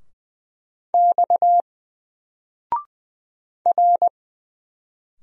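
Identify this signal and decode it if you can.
Morse code, heard as a keyed tone, about 20 wpm, 700 Hz. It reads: X R
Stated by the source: Single letter and number Morse drill at 20 wpm, from N4LCD.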